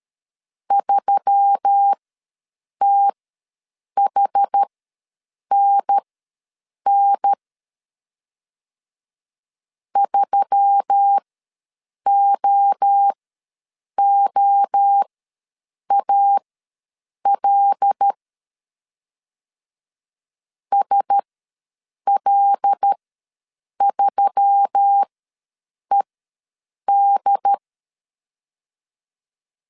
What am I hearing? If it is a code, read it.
Morse code, heard as a keyed tone, about 13 wpm, spaced as for 7 wpm, 800 Hz. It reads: 3THNN 3OOAL SL3ED
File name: LEZIONI / 5 LBVC / 5 L ws 12wpm 2958